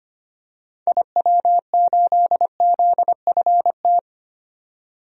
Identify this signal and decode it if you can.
Morse code, heard as a keyed tone, about 25 wpm, 700 Hz. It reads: IW8ZFT
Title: DXCC-100 Call Signs - 6 Characters - 25wpm